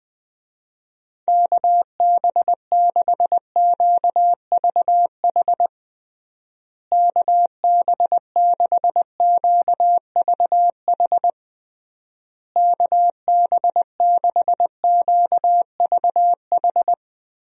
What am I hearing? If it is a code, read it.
Morse code, heard as a keyed tone, about 21 wpm, 700 Hz. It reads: KB6QVH KB6QVH KB6QVH
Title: USA Call Signs - 6 Characters - Repeated 3x - 20wpm